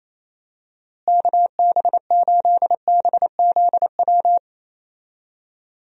Morse code, heard as a keyed tone, about 28 wpm, 700 Hz. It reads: KB8BZW
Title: USA Call Signs - 6 Characters - Repeated 3x - 28wpm